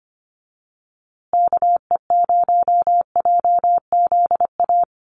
Morse code, heard as a keyed tone, about 25 wpm, 700 Hz. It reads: KE0JZA
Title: USA Call Signs Repeated 2x 25wpm